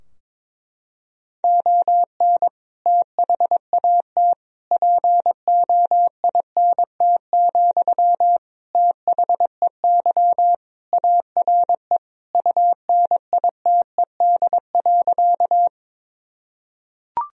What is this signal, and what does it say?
Morse code, heard as a keyed tone, about 22 wpm, 700 Hz. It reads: ON THAT POINT, THEY ARE UNITED.
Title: Sentences from Top 1000 Words - Review Part 4 - 22wpm